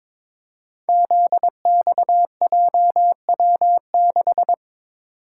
Morse code, heard as a keyed tone, about 22 wpm, 700 Hz. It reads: ZXJW6
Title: Code Groups - 5 Letter:Numbers - 22wpm